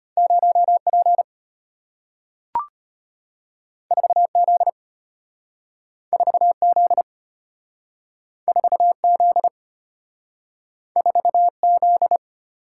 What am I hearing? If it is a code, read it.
Morse code, heard as a keyed tone, about 34 wpm, 700 Hz. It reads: 0P 4Z 4Z 4Z 4Z